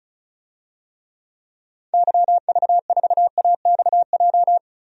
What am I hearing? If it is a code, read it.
Morse code, heard as a keyed tone, about 34 wpm, 700 Hz. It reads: YV4AXJ
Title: DXCC-100 Call Signs - 6 Characters - 35wpm